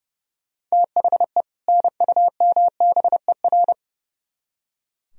Morse code, heard as a keyed tone, about 37 wpm, 700 Hz. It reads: THE NUMBER
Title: Speed-Racing - Sets of 2 Words 30wpm